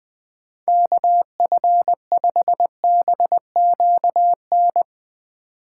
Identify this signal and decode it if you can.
Morse code, heard as a keyed tone, about 20 wpm, 700 Hz. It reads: KF5BQN